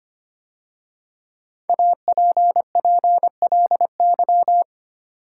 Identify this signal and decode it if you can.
Morse code, heard as a keyed tone, about 25 wpm, 700 Hz. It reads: APPLY
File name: 5 Letter Words - First 3 Letters Given - 25wpm